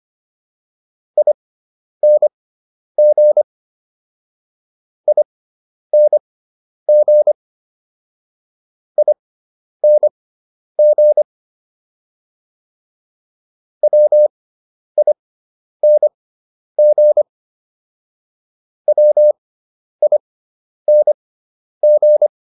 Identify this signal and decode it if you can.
Morse code, heard as a keyed tone, about 25 wpm, spaced as for 10 wpm, 600 Hz. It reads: ING ING ING WING WING